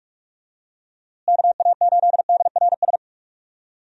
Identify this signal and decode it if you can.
Morse code, heard as a keyed tone, about 46 wpm, 700 Hz. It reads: KA8DRS
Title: USA Call Signs - 6 Characters - Repeated 2x - 45wpm